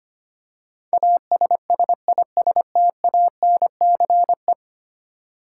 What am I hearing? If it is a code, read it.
Morse code, heard as a keyed tone, about 25 wpm, 700 Hz. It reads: ASSISTANCE
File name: 10 Letter Words - First 3 Letters Given - 25wpm